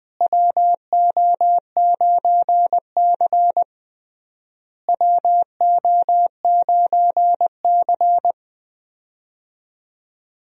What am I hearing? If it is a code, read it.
Morse code, heard as a keyed tone, about 20 wpm, 700 Hz. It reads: WO9C WO9C
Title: USA Call Signs - 4 Characters - Repeated 3x - 20wpm